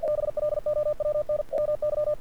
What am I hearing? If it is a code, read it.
Morse code, heard as a keyed tone, about 47 wpm, 590 Hz. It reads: BLOWN MY
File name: archived / CRSnet